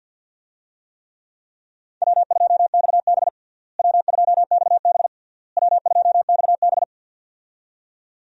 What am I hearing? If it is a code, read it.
Morse code, heard as a keyed tone, about 52 wpm, 700 Hz. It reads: W2XB W2XB W2XB
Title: USA Call Signs - 4 Characters - Repeated 3x - 50wpm